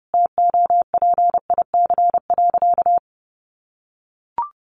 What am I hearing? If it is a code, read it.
Morse code, heard as a keyed tone, about 30 wpm, 700 Hz. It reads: TOPIC.
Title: Sentences from Top 1200 Words 30wpm